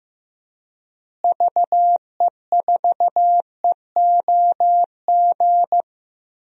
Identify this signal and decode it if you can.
Morse code, heard as a keyed tone, about 15 wpm, 700 Hz. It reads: VE4EOG
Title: DXCC-100 Call Signs - 6 Characters - Repeated 3x - 15wpm